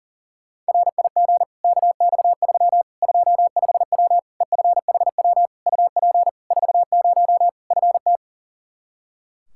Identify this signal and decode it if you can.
Morse code, heard as a keyed tone, about 40 wpm, 700 Hz. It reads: RIG KX3 25W EFHW UP 40 FT
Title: Practice QSO - Exchange 2 - Info - 40wpm